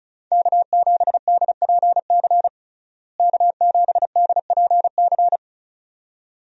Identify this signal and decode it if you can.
Morse code, heard as a keyed tone, about 36 wpm, 700 Hz. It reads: K7DPC K7DPC
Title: USA Call Signs - 5 Characters - Repeated 2x - 35wpm